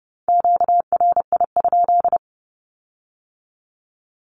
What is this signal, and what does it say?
Morse code, heard as a keyed tone, about 30 wpm, 690 Hz. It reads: QRI?
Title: Q-Codes with Spoken Meaning - 30wpm